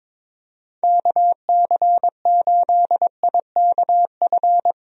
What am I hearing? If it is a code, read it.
Morse code, heard as a keyed tone, about 21 wpm, 700 Hz. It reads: KC8IKF
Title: USA Call Signs - 6 Characters - Repeated 3x - 22wpm